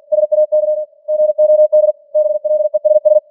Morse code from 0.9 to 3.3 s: GOD BLESS